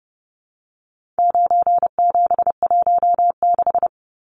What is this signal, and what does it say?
Morse code, heard as a keyed tone, about 30 wpm, 700 Hz. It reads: 9716
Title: Numbers - 4-Digit 30wpm